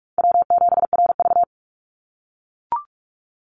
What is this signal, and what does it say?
Morse code, heard as a keyed tone, about 44 wpm, 700 Hz. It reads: W7RV
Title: NA SOTA Chaser Callsigns 45wpm